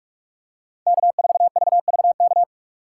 Morse code, heard as a keyed tone, about 46 wpm, 700 Hz. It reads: K4VVK
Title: USA Call Signs - 5 Characters - 3x Character Spacing - 45wpm